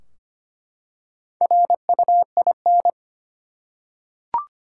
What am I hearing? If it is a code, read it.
Morse code, heard as a keyed tone, about 25 wpm, 700 Hz. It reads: RUIN
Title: Four Letter Words - 25wpm